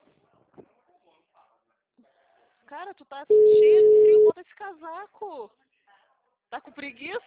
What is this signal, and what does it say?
Ringback tone (Europe, 425 Hz), with speech.
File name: out- -2025- - - .13198